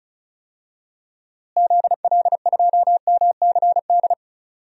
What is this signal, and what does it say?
Morse code, heard as a keyed tone, about 36 wpm, 700 Hz. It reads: ZL2MCD
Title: DXCC-100 Call Signs - 6 Characters - Repeated 2x - 35wpm